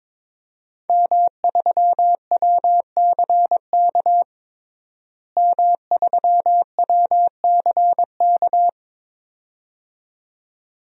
Morse code, heard as a keyed tone, about 22 wpm, 700 Hz. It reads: M3WCK M3WCK
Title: DXCC-100 Call Signs - 5 Characters - Repeated 2x - 22wpm